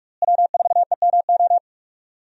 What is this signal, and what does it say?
Morse code, heard as a keyed tone, about 44 wpm, 700 Hz. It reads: W4EMO
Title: USA Call Signs - 5 Characters - Repeated 2x - 45wpm